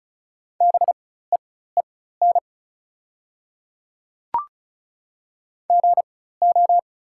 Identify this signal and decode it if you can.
Morse code, heard as a keyed tone, about 36 wpm, spaced as for 16 wpm, 700 Hz. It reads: BEEN GO
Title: 4x Character Spacing - Top 200 Words - 35wpm